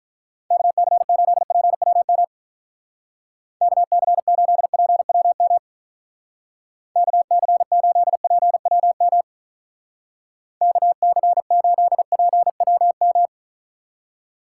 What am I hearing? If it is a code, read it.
Morse code, heard as a keyed tone, about 43 wpm, 700 Hz. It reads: KC8PWM KC8PWM KC8PWM KC8PWM
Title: Speed Racing - USA 2x3 Call Signs 35wpm